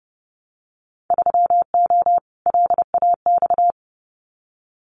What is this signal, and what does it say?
Morse code, heard as a keyed tone, about 30 wpm, 700 Hz. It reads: 3O LAX
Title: Winter Field Day Exchanges Encore 30wpm